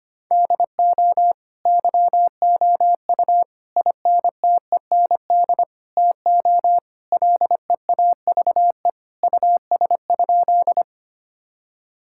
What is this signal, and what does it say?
Morse code, heard as a keyed tone, about 25 wpm, 700 Hz. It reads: DO YOU INTEND TO LEAVE US?
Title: Sentences from Top 1600 Words 25wpm